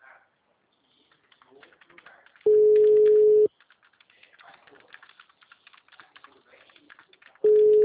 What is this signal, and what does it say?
Ringback tone (Europe, 425 Hz), with speech.